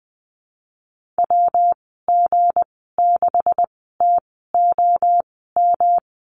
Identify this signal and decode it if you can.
Morse code, heard as a keyed tone, about 19 wpm, spaced as for 14 wpm, 700 Hz. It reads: WG6TOM
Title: 173 Lesson 28g - Number 9 - Call Signs 20wpm